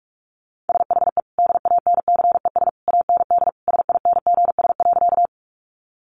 Morse code, heard as a keyed tone, about 46 wpm, 700 Hz. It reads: SHE DANCES AND SINGS.
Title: Easy Phrases 45wpm